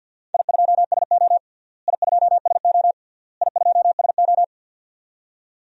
Morse code, heard as a keyed tone, about 50 wpm, 700 Hz. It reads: I2SO I2SO I2SO
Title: DXCC-100 Call Signs - 4 Characters - Repeated 3x - 50wpm